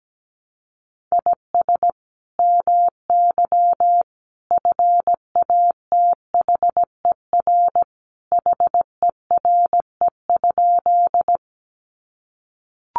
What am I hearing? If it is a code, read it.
Morse code, heard as a keyed tone, about 17 wpm, 700 Hz. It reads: IS MY FATHER HERE?
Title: Sentences from Top 300 Words 17wpm